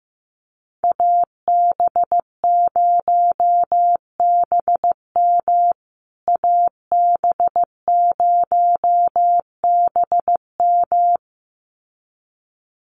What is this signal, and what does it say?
Morse code, heard as a keyed tone, about 15 wpm, 700 Hz. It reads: AB0BM AB0BM
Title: USA Call Signs Repeated 2x 15wpm